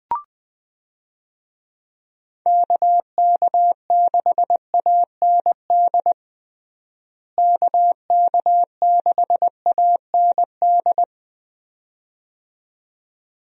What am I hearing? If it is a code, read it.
Morse code, heard as a keyed tone, about 20 wpm, 700 Hz. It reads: KK6AND KK6AND
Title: USA Call Signs - 6 Characters - Repeated 2x - 20wpm